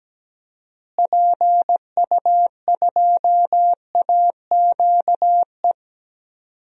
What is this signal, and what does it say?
Morse code, heard as a keyed tone, about 17 wpm, 700 Hz. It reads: PU2AQE
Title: DXCC-100 Call Signs - 6 Characters - Repeated 2x - 17wpm